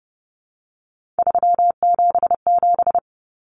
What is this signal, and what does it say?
Morse code, heard as a keyed tone, about 30 wpm, 700 Hz. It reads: 377
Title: Common Cut Numbers - 3-Digit 30wpm